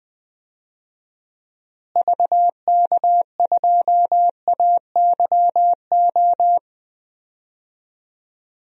Morse code, heard as a keyed tone, about 20 wpm, 700 Hz. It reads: VK2AYO
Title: DXCC-100 Call Signs - 6 Characters - Repeated 3x - 20wpm